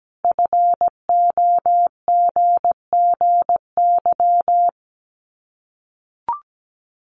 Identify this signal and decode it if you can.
Morse code, heard as a keyed tone, about 17 wpm, 700 Hz. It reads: FOGGY
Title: Common QSO Phrases 17wpm